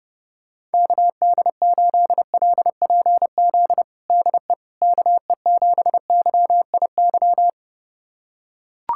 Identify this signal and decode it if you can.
Morse code, heard as a keyed tone, about 32 wpm, 700 Hz. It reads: KD8LPZ DE KE7YIY
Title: USA Callsign DE Callsign - 6 Characters - 30wpm